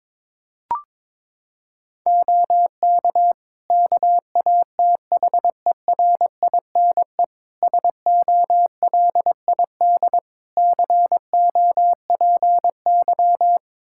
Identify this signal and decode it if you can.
Morse code, heard as a keyed tone, about 22 wpm, 700 Hz. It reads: OK KATHERINE SOLID COPY
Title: K1USN SST QSO Practice 22wpm